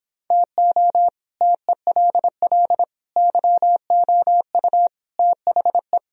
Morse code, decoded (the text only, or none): TO TELL YOU THE